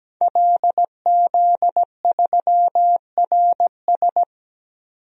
Morse code, heard as a keyed tone, about 17 wpm, 700 Hz. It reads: LZ3RS
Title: DXCC-100 Call Signs - 5 Characters - 17wpm